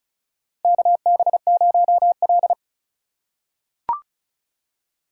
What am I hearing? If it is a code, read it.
Morse code, heard as a keyed tone, about 36 wpm, 700 Hz. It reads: KB0L
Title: USA Call Signs - 4 Characters - 2x Character Spacing - 35wpm